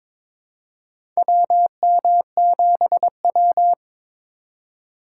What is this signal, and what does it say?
Morse code, heard as a keyed tone, about 22 wpm, 700 Hz. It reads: WM7W
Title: USA Call Signs - 4 Characters - 3x Character Spacing - 22wpm